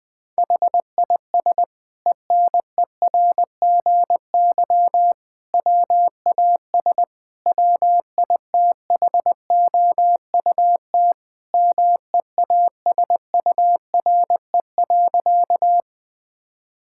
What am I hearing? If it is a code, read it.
Morse code, heard as a keyed tone, about 21 wpm, 700 Hz. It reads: HIS ENERGY WAS WITHOUT MEASURE.
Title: Sentences from Top 1000 Words - Review Part 6 - 20wpm